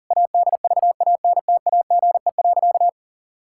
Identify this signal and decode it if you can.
Morse code, heard as a keyed tone, about 40 wpm, 700 Hz. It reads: ADVANTAGE.
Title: Sentences from Top 1000 Words - Review Part 6 - 40wpm